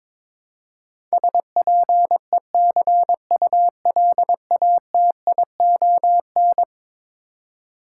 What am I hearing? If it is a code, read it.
Morse code, heard as a keyed tone, about 22 wpm, 700 Hz. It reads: SPECULATION
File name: practice / 22wpm / Eleven Letter Words - 22wpm